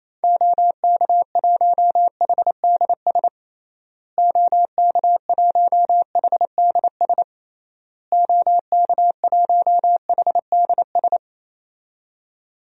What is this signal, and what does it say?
Morse code, heard as a keyed tone, about 28 wpm, 700 Hz. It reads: OK1HDS OK1HDS OK1HDS
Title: DXCC-100 Call Signs - 6 Characters - Repeated 3x - 28wpm